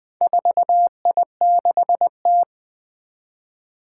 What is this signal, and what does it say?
Morse code, heard as a keyed tone, about 20 wpm, 700 Hz. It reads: 4I6T
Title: Four Challenging Letter-Number Combinations 20wpm